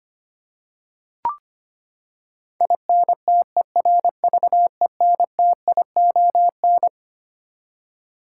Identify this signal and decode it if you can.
Morse code, heard as a keyed tone, about 25 wpm, 700 Hz. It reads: INTERVENTION
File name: Twelve Letter Words - 25wpm